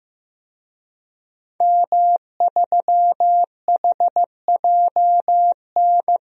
Morse code, heard as a keyed tone, about 15 wpm, 700 Hz. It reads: M3HJN
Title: DXCC-100 Call Signs - 5 Characters - 15wpm